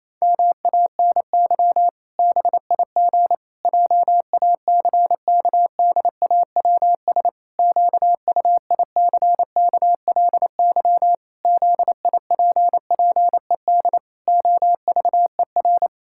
Morse code, heard as a keyed tone, about 28 wpm, 700 Hz. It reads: MANY BIG JACKDAWS QUICKLY ZIPPED OVER
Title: Pangram Challenge v2 - 28wpm